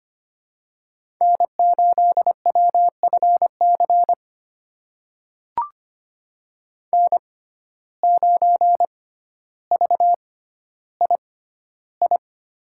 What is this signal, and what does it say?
Morse code, heard as a keyed tone, about 25 wpm, spaced as for 13 wpm, 700 Hz. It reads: N8WFC N9VII
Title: USA Call Signs - 5 Characters - 6x Character Spacing - 25wpm